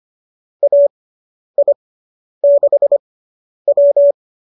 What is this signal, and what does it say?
Morse code, heard as a keyed tone, about 25 wpm, spaced as for 10 wpm, 570 Hz. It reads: AI6W